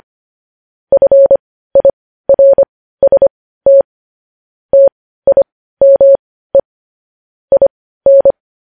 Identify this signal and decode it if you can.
Morse code, heard as a keyed tone, about 25 wpm, spaced as for 15 wpm, 550 Hz. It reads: FIRST TIME IN
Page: news headlines in morse code at 15 wpm